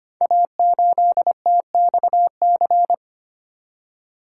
Morse code, heard as a keyed tone, about 25 wpm, 700 Hz. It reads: A8TXC
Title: Code Groups - 5 Letter:Numbers - 25wpm